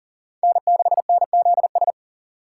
Morse code, heard as a keyed tone, about 40 wpm, 700 Hz. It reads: N6NZS
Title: USA Call Signs - 5 Characters - 3x Character Spacing - 40wpm